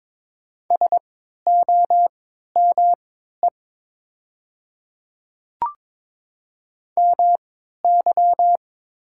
Morse code, heard as a keyed tone, about 22 wpm, spaced as for 13 wpm, 690 Hz. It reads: SOME MY
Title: 3x Character Spacing - Top 200 Words - 22wpm